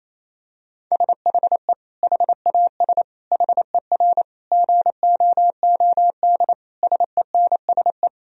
Morse code, heard as a keyed tone, about 28 wpm, 700 Hz. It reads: SHE HAS HER GOOD SENSE
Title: Sentences from Top 1000 Words - Review Part 4 - 28wpm